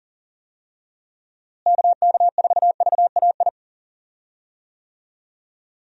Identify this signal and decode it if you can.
Morse code, heard as a keyed tone, about 38 wpm, 700 Hz. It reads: KK4VAI
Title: USA Call Signs - 6 Characters - Repeated 2x - 40wpm